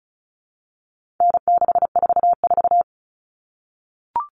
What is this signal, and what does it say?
Morse code, heard as a keyed tone, about 34 wpm, 700 Hz. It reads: N644
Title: All Cut Numbers - 4-Digit 35wpm